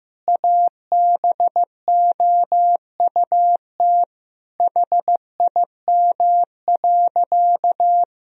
Morse code, heard as a keyed tone, about 15 wpm, 700 Hz. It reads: ABOUT HIM.
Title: Sentences from Top 1000 Words - Review Part 1 - 15wpm